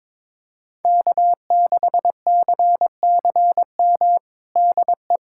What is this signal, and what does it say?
Morse code, heard as a keyed tone, about 22 wpm, 700 Hz. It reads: K6CCM DE
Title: USA Callsign DE Callsign - 5 Characters - 22wpm